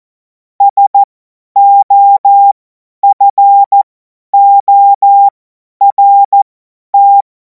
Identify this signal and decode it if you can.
Morse code, heard as a keyed tone, about 14 wpm, spaced as for 10 wpm, 800 Hz. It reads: SOFORT
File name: Morsekurs / Witze / Witz 17. März 2026 50 F70